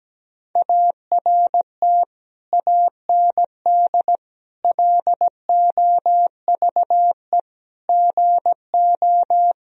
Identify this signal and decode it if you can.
Morse code, heard as a keyed tone, about 17 wpm, 700 Hz. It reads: ART AND LOVE GO
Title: Sentences from Top 1000 Words - Review Part 3 - 17wpm